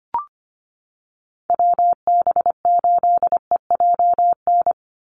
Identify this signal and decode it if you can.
Morse code, heard as a keyed tone, about 25 wpm, 700 Hz. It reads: WB8EJN
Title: Call Signs - 25wpm